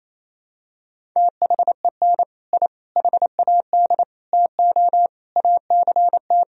Morse code, heard as a keyed tone, about 28 wpm, 700 Hz. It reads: THEN I HAD TO ACT
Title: Sentences from Top 1000 Words - Review Part 1 - 28wpm